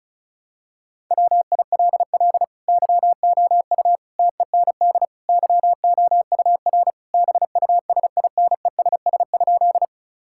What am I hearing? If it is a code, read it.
Morse code, heard as a keyed tone, about 36 wpm, 700 Hz. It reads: WILL YOU TEND YOUR BUSINESS?
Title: Sentences from Top 1000 Words - Review Part 5 - 35wpm